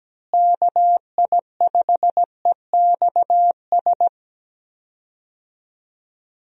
Morse code, heard as a keyed tone, about 17 wpm, 700 Hz. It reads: KI5EXS
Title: USA Call Signs - 6 Characters - Repeated 3x - 17wpm